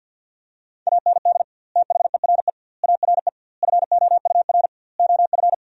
Morse code, heard as a keyed tone, about 50 wpm, 700 Hz. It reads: AND THERE ARE FOUR OF
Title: 2x Word-Spacing - Sentences from Top 300 Words - 50wpm